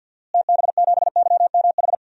Morse code, heard as a keyed tone, about 51 wpm, 700 Hz. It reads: TB7YMH
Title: DXCC-100 Call Signs - 6 Characters - Repeated 3x - 50wpm